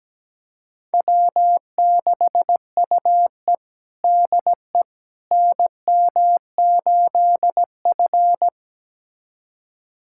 Morse code, heard as a keyed tone, about 17 wpm, 700 Hz. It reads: W6UE DE NM8F
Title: USA Callsign DE Callsign - 4 Characters - 17wpm